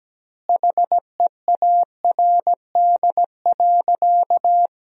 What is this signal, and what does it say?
Morse code, heard as a keyed tone, about 17 wpm, 700 Hz. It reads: HEARD.
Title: 2x Word-Spacing - Sentences from Top 400 Words - 17wpm